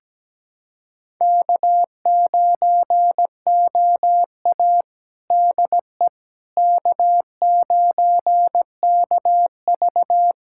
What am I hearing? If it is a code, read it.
Morse code, heard as a keyed tone, about 17 wpm, 700 Hz. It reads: K9OA DE K9KV
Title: USA Callsign DE Callsign - 4 Characters - 17wpm